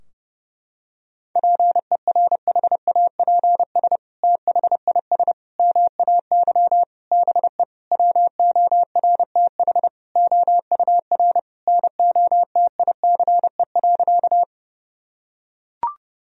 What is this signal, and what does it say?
Morse code, heard as a keyed tone, about 30 wpm, 700 Hz. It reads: PERHAPS THIS MAY BE WORTH OUR NOTICE.
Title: Sentences from Top 1000 Words - Review Part 2 - 30wpm